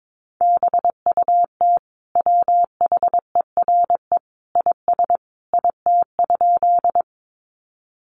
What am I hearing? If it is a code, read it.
Morse code, heard as a keyed tone, about 22 wpm, 700 Hz. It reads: BUT WHERE IS IT?
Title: Sentences from Top 200 Words 22wpm